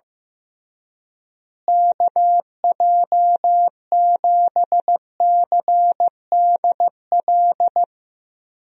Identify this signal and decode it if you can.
Morse code, heard as a keyed tone, about 16 wpm, 700 Hz. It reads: KJ7CDL
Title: USA Call Signs - 6 Characters - Repeated 3x - 15wpm